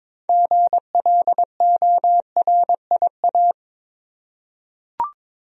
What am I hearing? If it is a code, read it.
Morse code, heard as a keyed tone, about 21 wpm, 700 Hz. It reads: GLORIA
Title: USA Common Names - 22wpm